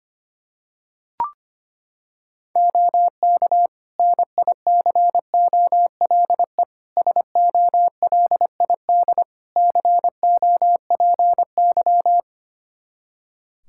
Morse code, heard as a keyed tone, about 25 wpm, 700 Hz. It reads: OK NICOLE SOLID COPY
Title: Practice QSO - Exchange 3 - More Info - 25wpm